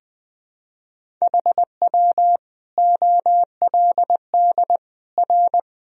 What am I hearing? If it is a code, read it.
Morse code, heard as a keyed tone, about 21 wpm, 700 Hz. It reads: HW OLD R
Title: Sentences from Top 200 Words and Abbreviations 20wpm